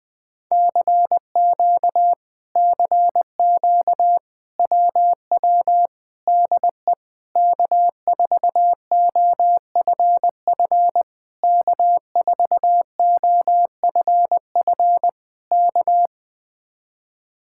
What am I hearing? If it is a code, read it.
Morse code, heard as a keyed tone, about 20 wpm, 700 Hz. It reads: CQ CQ WW DE K4OFF K4OFF K
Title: Speed-Racing Common QSO Phrases - 20wpm